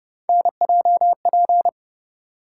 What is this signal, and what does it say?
Morse code, heard as a keyed tone, about 29 wpm, 700 Hz. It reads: NJP